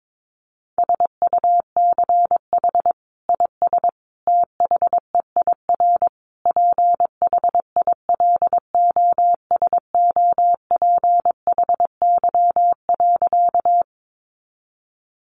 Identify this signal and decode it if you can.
Morse code, heard as a keyed tone, about 22 wpm, 700 Hz. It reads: SUCH IS THEIR PHILOSOPHY.
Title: Sentences from Top 900 Words 22wpm